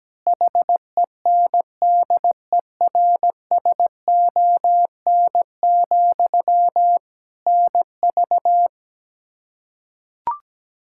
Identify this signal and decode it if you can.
Morse code, heard as a keyed tone, about 17 wpm, 700 Hz. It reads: HENDERSON, NV
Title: Practice QSO - Exchange 1 - Beginning - 17wpm